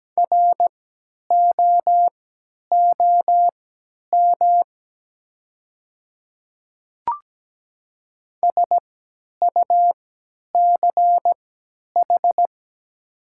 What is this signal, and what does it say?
Morse code, heard as a keyed tone, about 17 wpm, spaced as for 10 wpm, 700 Hz. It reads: ROOM SUCH
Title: 3x Character Spacing - Top 300 Words - 17wpm